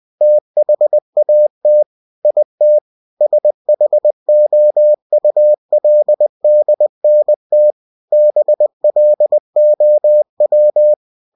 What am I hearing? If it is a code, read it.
Morse code, heard as a keyed tone, about 20 wpm, 600 Hz. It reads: THAT IT SHOULDNT BLOW